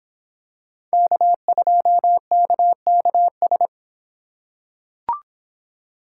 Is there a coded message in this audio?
Yes: Morse code from 0.9 to 3.7 s, heard as a keyed tone, about 25 wpm, 700 Hz. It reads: K2KKS